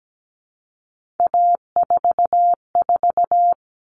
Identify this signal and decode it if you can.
Morse code, heard as a keyed tone, about 17 wpm, 700 Hz. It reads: A44
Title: All Cut Numbers - 3-Digit 17wpm